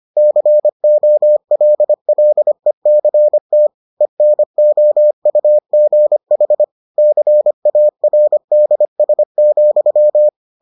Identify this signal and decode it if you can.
Morse code, heard as a keyed tone, about 25 wpm, 600 Hz. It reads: COLLECT ENOUGH CARDS,